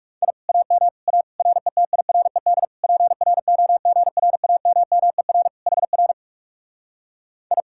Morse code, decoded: I AM A RETIRED PROGRAMMER HR I

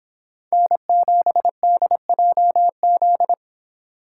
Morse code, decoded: N7DJZ

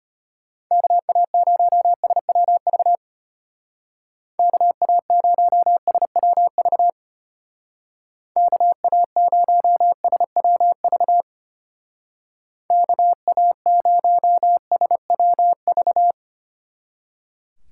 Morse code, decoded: KA0SWV KA0SWV KA0SWV KA0SWV

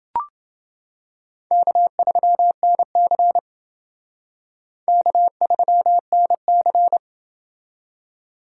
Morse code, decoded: K3NC K3NC

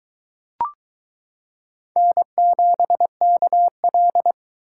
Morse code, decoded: N7KL

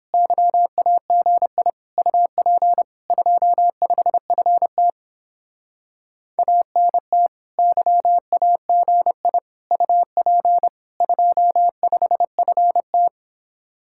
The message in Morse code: YAGI UP 25FT ANT YAGI UP 25FT

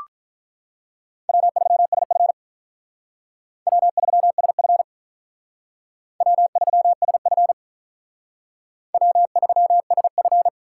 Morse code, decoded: W3SF W3SF W3SF W3SF

